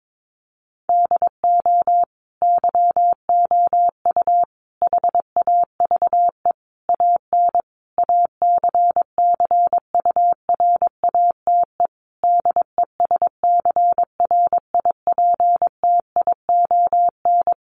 Morse code, heard as a keyed tone, about 22 wpm, 700 Hz. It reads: DO YOU HAVE AN ACCURATE DESCRIPTION
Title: Sentences from Top 1300 Words 22wpm